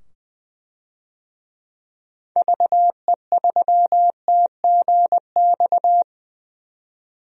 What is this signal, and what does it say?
Morse code, heard as a keyed tone, about 20 wpm, 700 Hz. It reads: VE3TGX
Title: DXCC-100 Call Signs - 6 Characters - 20wpm